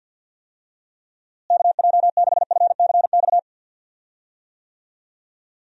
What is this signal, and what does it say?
Morse code, heard as a keyed tone, about 49 wpm, 700 Hz. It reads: KJ6FCX
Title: USA Call Signs - 6 Characters - Repeated 3x - 50wpm